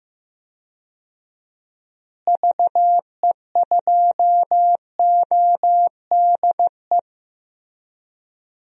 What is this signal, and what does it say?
Morse code, heard as a keyed tone, about 15 wpm, 700 Hz. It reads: VE2ODE
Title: DXCC-100 Call Signs - 6 Characters - Repeated 2x - 15wpm